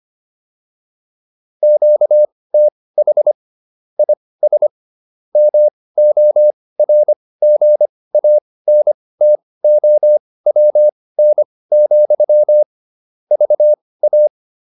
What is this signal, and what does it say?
Morse code, heard as a keyed tone, about 25 wpm, spaced as for 18 wpm, 600 Hz. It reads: QTH IS MORGANTOWN, VA
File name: QSO302 18